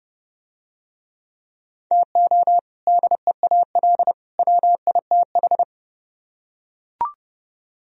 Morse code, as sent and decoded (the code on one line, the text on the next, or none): - --- / -.. .- .-.. / .-- .. - ....
TO DEAL WITH